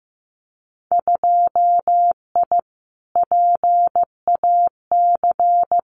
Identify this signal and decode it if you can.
Morse code, heard as a keyed tone, about 15 wpm, 700 Hz. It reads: 2I PAC
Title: Winter Field Day Exchanges Encore 15wpm